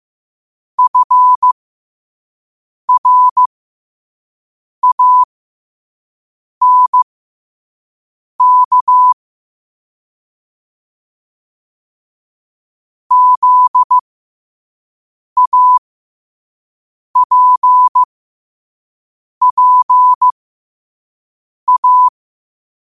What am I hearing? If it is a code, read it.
Morse code, heard as a keyed tone, about 15 wpm, spaced as for 5 wpm, 1000 Hz. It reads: FRANK ZAPPA
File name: qotdfiles / 5wpm